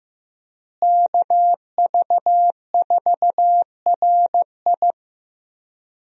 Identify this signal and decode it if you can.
Morse code, heard as a keyed tone, about 15 wpm, 700 Hz. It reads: KV4RI
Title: USA Call Signs - 5 Characters - Repeated 3x - 15wpm